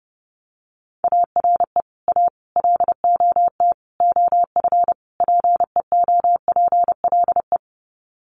Morse code, heard as a keyed tone, about 30 wpm, 700 Hz. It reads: ARE A LOT OF PEOPLE
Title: Sets of 5 Words 30wpm